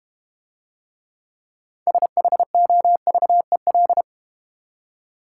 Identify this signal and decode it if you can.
Morse code, heard as a keyed tone, about 32 wpm, 700 Hz. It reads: SHOVEL